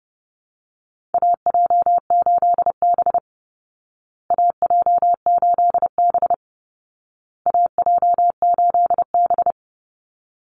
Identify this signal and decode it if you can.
Morse code, heard as a keyed tone, about 30 wpm, 700 Hz. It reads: AJ8B AJ8B AJ8B
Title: Call Signs Repeated 2x - 30wpm